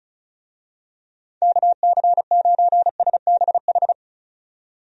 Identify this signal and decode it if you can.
Morse code, heard as a keyed tone, about 36 wpm, 700 Hz. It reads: KC9SBH
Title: USA Call Signs - 6 Characters - 35wpm